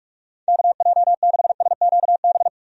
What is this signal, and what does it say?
Morse code, heard as a keyed tone, about 46 wpm, 700 Hz. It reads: KJ6SQB
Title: USA Call Signs - 6 Characters - 45wpm